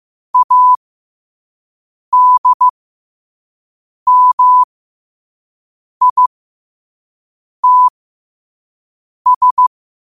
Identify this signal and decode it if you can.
Morse code, heard as a keyed tone, about 15 wpm, spaced as for 5 wpm, 1000 Hz. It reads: ADMITS
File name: qotdfiles / 5wpm